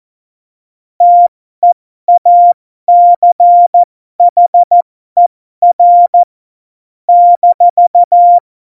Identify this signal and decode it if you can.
Morse code, heard as a keyed tone, about 14 wpm, spaced as for 12 wpm, 700 Hz. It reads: TEACHER -
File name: Morsekurs / Witze